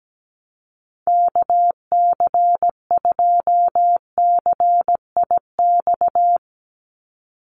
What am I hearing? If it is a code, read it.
Morse code, heard as a keyed tone, about 17 wpm, 700 Hz. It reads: KC2CIX
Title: USA Call Signs 17wpm